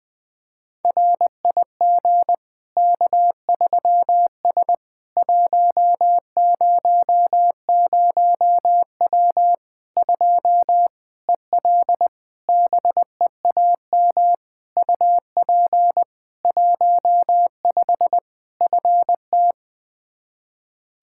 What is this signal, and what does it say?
Morse code, heard as a keyed tone, about 21 wpm, 700 Hz. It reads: RIG K3S 100W 2 EL BEAM UP 15 FT
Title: Practice QSO - Exchange 2 - Info - 20wpm